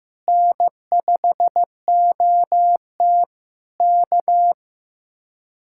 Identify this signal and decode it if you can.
Morse code, heard as a keyed tone, about 15 wpm, 700 Hz. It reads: N5OT K
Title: Speed-Racing Common QSO Phrases Encore - 15wpm